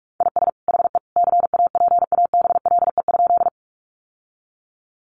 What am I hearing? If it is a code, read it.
Morse code, heard as a keyed tone, about 44 wpm, 700 Hz. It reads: IS HE CAPABLE?
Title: Sentences from Top 1200 Words 45wpm